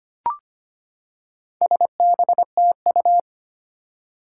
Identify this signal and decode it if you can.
Morse code, heard as a keyed tone, about 25 wpm, 700 Hz. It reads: SBTU
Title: Four Challenging Letter-Number Combinations 25wpm